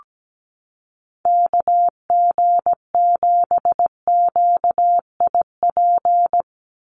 Morse code, heard as a keyed tone, about 17 wpm, 700 Hz. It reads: KG7QIP
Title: USA Call Signs 17wpm